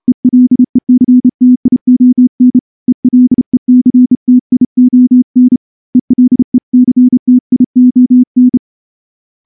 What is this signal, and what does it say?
Morse code, heard as a keyed tone, about 29 wpm, 250 Hz. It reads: ELECTION ELECTION ELECTION
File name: morse-training-30WPM